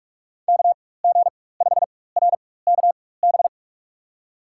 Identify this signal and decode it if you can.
Morse code, heard as a keyed tone, about 44 wpm, spaced as for 21 wpm, 700 Hz. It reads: KG5RKB
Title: USA Call Signs - 6 Characters - 4x Character Spacing - 45wpm